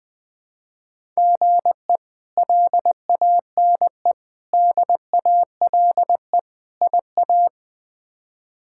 Morse code, heard as a keyed tone, about 20 wpm, 700 Hz. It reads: GE LANE DALE IA
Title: K1USN SST QSO Practice - 20wpm